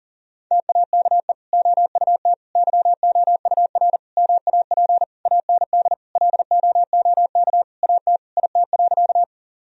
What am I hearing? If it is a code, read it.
Morse code, heard as a keyed tone, about 40 wpm, 700 Hz. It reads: TAKE OUT YOUR MAP AND LOOK AT IT.
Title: Sentences from Top 1000 Words - Review Part 2 - 40wpm